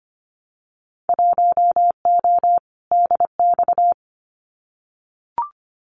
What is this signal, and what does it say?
Morse code, heard as a keyed tone, about 25 wpm, 700 Hz. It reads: 1O DX
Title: Winter Field Day Exchanges Encore 25wpm